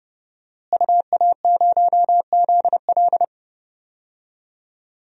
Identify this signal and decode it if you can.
Morse code, heard as a keyed tone, about 30 wpm, 700 Hz. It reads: UA0ZL
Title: DXCC-100 Call Signs - 5 Characters - Repeated 3x - 30wpm